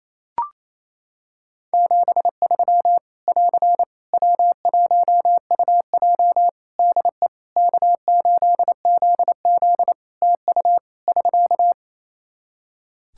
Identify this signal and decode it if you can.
Morse code, heard as a keyed tone, about 28 wpm, 700 Hz. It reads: 73 <AR> W1UJ DE K8ZZ TU <SK>
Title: Practice QSO - Exchange 4 - Ending - 28wpm